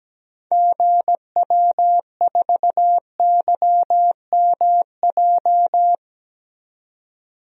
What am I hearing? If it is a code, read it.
Morse code, heard as a keyed tone, about 17 wpm, 700 Hz. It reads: GW4YMJ
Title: DXCC-100 Call Signs - 6 Characters - 17wpm